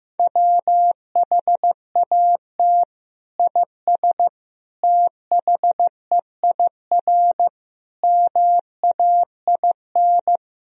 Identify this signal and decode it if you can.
Morse code, heard as a keyed tone, about 15 wpm, 700 Hz. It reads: WHAT IS THEIR MAIN
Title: Sentences from Top 1000 Words - Review Part 2 - 15wpm